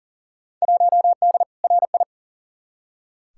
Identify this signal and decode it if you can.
Morse code, heard as a keyed tone, about 40 wpm, 700 Hz. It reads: 1D RI
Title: Field Day Exchanges - 40wpm